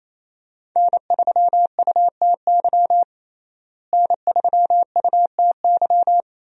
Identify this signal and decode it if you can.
Morse code, heard as a keyed tone, about 27 wpm, 700 Hz. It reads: N3UTY N3UTY
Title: USA Call Signs - 5 Characters - Repeated 2x - 28wpm